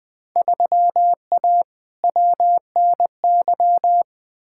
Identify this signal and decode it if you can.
Morse code, heard as a keyed tone, about 20 wpm, 700 Hz. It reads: 3A WNY
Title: Field Day Exchanges - 20wpm